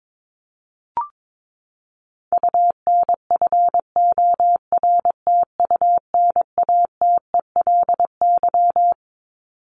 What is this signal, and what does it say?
Morse code, heard as a keyed tone, about 22 wpm, 700 Hz. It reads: UNFORTUNATELY
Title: LY Suffix - Top 100 Words with Base Words 22wpm